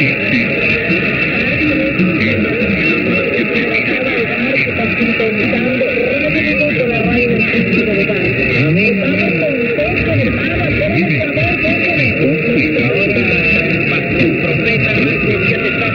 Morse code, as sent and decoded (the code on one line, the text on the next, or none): none